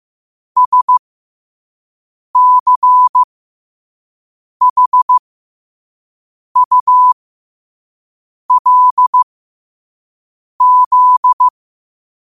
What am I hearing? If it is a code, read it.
Morse code, heard as a keyed tone, about 15 wpm, spaced as for 5 wpm, 1000 Hz. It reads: SCHULZ
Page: New quotes every day in morse code at 5 Words per minute.